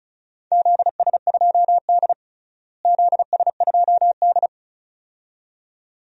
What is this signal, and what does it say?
Morse code, heard as a keyed tone, about 36 wpm, 700 Hz. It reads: ZS2D ZS2D
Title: DXCC-100 Call Signs - 4 Characters - Repeated 2x - 35wpm